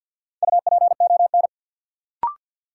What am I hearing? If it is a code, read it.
Morse code, heard as a keyed tone, about 53 wpm, 700 Hz. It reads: UPON
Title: Sentences from Top 1000 Words - Review Part 6 - New Word Drill - 50wpm